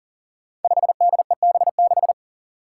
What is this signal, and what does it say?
Morse code, heard as a keyed tone, about 40 wpm, 700 Hz. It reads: 5DEB6
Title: Five Challenging Letter-Number Combinations 40wpm